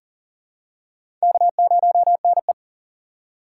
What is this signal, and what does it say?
Morse code, heard as a keyed tone, about 40 wpm, 690 Hz. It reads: K0NE
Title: USA Call Signs - 4 Characters - 2x Character Spacing - 40wpm